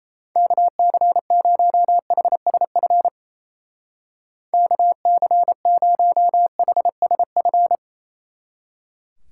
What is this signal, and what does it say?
Morse code, heard as a keyed tone, about 31 wpm, 700 Hz. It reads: KC0HSF KC0HSF
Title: Speed Racing - USA Call Signs 28wpm